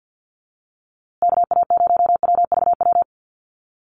Morse code, heard as a keyed tone, about 50 wpm, 700 Hz. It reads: KA0WVW
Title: USA Call Signs Repeated 3x 50wpm